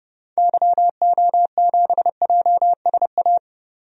Morse code, heard as a keyed tone, about 31 wpm, 700 Hz. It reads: YO7JSA
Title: DXCC-100 Call Signs - 6 Characters - 30wpm